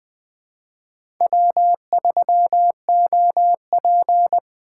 Morse code, heard as a keyed tone, about 21 wpm, 700 Hz. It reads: W3OP